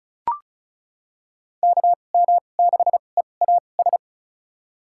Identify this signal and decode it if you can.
Morse code, heard as a keyed tone, about 34 wpm, spaced as for 25 wpm, 700 Hz. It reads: KM6EAS
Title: USA Call Signs - 6 Characters - 2x Character Spacing - 35wpm